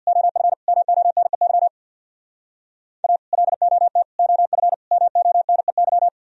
Morse code, heard as a keyed tone, about 64 wpm, 700 Hz. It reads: OF MONEY A LOT OF MONEY